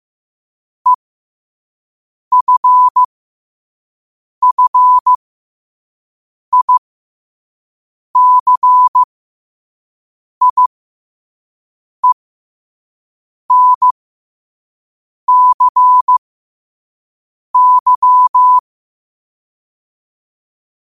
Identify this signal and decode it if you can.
Morse code, heard as a keyed tone, about 15 wpm, spaced as for 5 wpm, 1000 Hz. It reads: EFFICIENCY
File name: qotdfiles / 5wpm